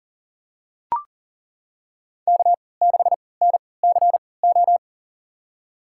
Morse code, heard as a keyed tone, about 38 wpm, spaced as for 22 wpm, 700 Hz. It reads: K6NCO
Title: USA Call Signs - 5 Characters - 3x Character Spacing - 40wpm